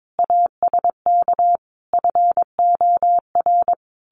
Morse code, heard as a keyed tone, about 22 wpm, 700 Hz. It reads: ASK FOR
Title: Sentences from Top 1500 Words 22wpm